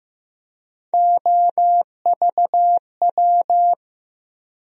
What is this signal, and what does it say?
Morse code, heard as a keyed tone, about 15 wpm, 700 Hz. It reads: OVW
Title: USA Call Sign Postfixes - 3 Letters 15wpm